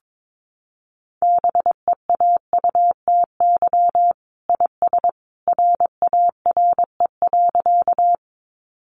Morse code, heard as a keyed tone, about 22 wpm, 700 Hz. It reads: BEAUTY IS RARE.
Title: Sentences from Top 1100 Words 22wpm